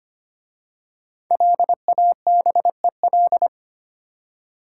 Morse code, heard as a keyed tone, about 25 wpm, 700 Hz. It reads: LABEL
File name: 5 Letter Words - First 3 Letters Given - 25wpm